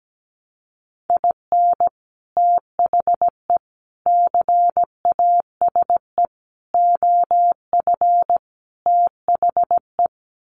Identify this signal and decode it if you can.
Morse code, heard as a keyed tone, about 17 wpm, 700 Hz. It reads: IN THE CASE OF THE
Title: Sets of 5 Words 17wpm